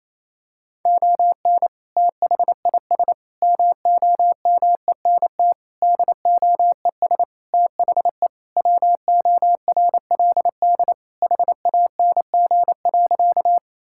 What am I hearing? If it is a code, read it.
Morse code, heard as a keyed tone, about 28 wpm, 700 Hz. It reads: ON THIS MOMENT DOES THE WORLD HANG.
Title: Sentences from Top 1600 Words 28wpm